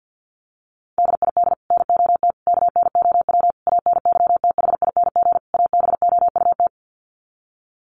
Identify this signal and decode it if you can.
Morse code, heard as a keyed tone, about 50 wpm, 700 Hz. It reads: DID NOT KNOW ANYTHING ABOUT
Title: Sets of 5 Words Encore2 50wpm